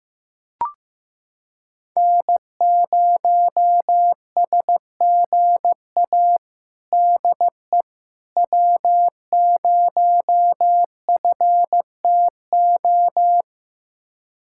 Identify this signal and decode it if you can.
Morse code, heard as a keyed tone, about 15 wpm, 700 Hz. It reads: N0SGA DE W0FTO
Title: USA Callsign DE Callsign - 5 Characters - 15wpm